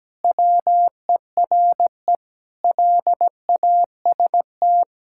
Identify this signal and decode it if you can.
Morse code, heard as a keyed tone, about 17 wpm, 700 Hz. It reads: WERE LAST
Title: Pangram Challenge v2 - 17wpm